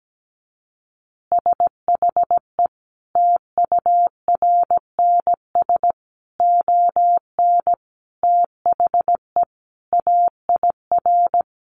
Morse code, heard as a keyed tone, about 17 wpm, 700 Hz. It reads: SHE TURNS ON THE AIR
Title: Additional Easy Phrases 17wpm